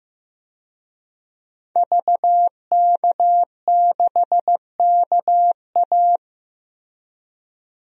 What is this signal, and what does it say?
Morse code, heard as a keyed tone, about 15 wpm, 700 Hz. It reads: VK6KA
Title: DXCC-100 Call Signs - 5 Characters - Repeated 2x - 15wpm